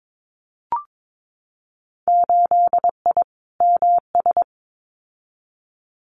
Morse code, heard as a keyed tone, about 22 wpm, 700 Hz. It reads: 8I MS
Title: Winter Field Day Exchanges Encore 22wpm